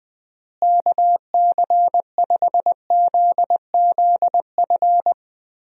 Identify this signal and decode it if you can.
Morse code, heard as a keyed tone, about 20 wpm, 700 Hz. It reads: KC5ZZF
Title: USA Call Signs - 6 Characters - Repeated 3x - 20wpm